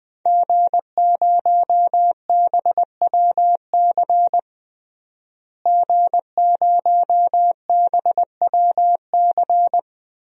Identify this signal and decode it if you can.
Morse code, heard as a keyed tone, about 21 wpm, 700 Hz. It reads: G0BWC G0BWC
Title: DXCC-100 Call Signs - 5 Characters - Repeated 3x - 20wpm